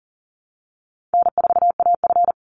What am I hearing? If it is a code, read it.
Morse code, heard as a keyed tone, about 40 wpm, 700 Hz. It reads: N4AF
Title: Easy Call Signs Repeated 3x - 40wpm